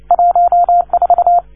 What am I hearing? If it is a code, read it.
Morse code, heard as a keyed tone, about 29 wpm, 690 Hz. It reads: 14
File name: 1esempio-telegrafia